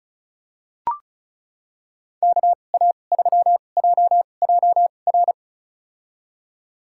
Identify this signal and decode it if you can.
Morse code, heard as a keyed tone, about 36 wpm, spaced as for 25 wpm, 700 Hz. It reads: KA3JJR
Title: USA Call Signs - 6 Characters - 2x Character Spacing - 35wpm